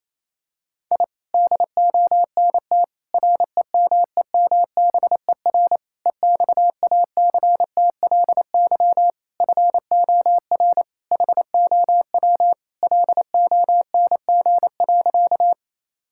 Morse code, heard as a keyed tone, about 28 wpm, 700 Hz. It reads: I DONT REMEMBER EXACTLY FOR HOW LONG.
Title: Sentences from Top 1000 Words - Review Part 6 - 28wpm